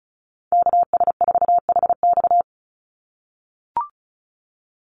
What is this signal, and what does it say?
Morse code, heard as a keyed tone, about 34 wpm, 700 Hz. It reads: KS4HX
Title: USA Call Signs 35wpm